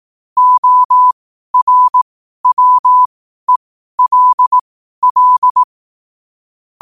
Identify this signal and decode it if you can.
Morse code, heard as a keyed tone, about 18 wpm, spaced as for 12 wpm, 1000 Hz. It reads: ORWELL
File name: qotdfiles / 12wpm